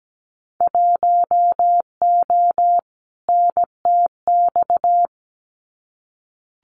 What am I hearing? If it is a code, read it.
Morse code, heard as a keyed tone, about 17 wpm, 700 Hz. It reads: 1O NTX
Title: Winter Field Day Exchanges 17wpm